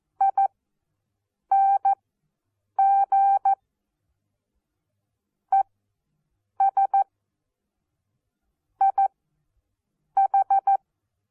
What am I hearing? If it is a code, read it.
Morse code, heard as a keyed tone, about 14 wpm, spaced as for 8 wpm, 790 Hz. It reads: I NG E S I H
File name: Onlinelehrgang / morsen / DJ4UF Morsen14